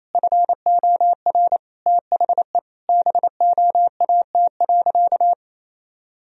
Morse code, heard as a keyed tone, about 28 wpm, 700 Hz. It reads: FOR THE BOAT.
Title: Sentences from Top 1000 Words - Review Part 3 - 28wpm